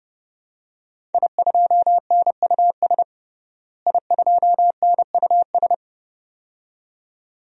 Morse code, heard as a keyed tone, about 30 wpm, 700 Hz. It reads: I2NUS I2NUS
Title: DXCC-100 Call Signs - 5 Characters - Repeated 2x - 30wpm